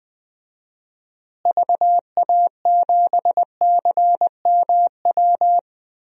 Morse code, heard as a keyed tone, about 20 wpm, 700 Hz. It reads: VA7CMW
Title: DXCC-100 Call Signs - 6 Characters - Repeated 3x - 20wpm